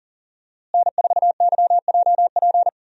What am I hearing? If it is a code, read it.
Morse code, heard as a keyed tone, about 40 wpm, 700 Hz. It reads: N4YJP
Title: USA Call Signs - 5 Characters - Repeated 3x - 40wpm